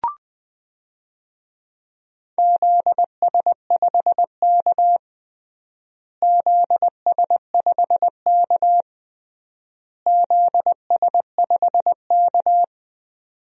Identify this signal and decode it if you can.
Morse code, heard as a keyed tone, about 20 wpm, 700 Hz. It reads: ZS5K ZS5K ZS5K
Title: DXCC-100 Call Signs - 4 Characters - Repeated 3x - 20wpm